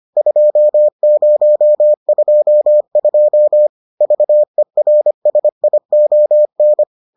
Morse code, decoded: 2022 VERSION

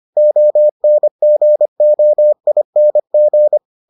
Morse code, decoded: ONGOING

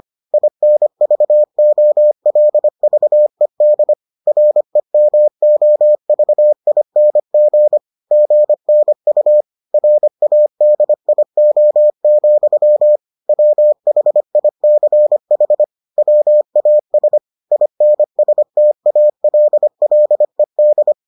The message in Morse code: INVOLVED REMOVING GNU RADIO, WHICH WAS INSTALLED

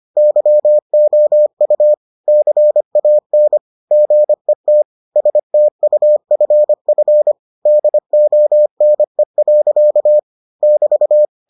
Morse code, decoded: YOU CAN GET STUFF DONE. <BT>

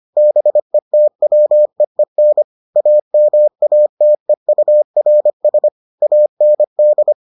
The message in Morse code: BETWEEN AMATEURS AND

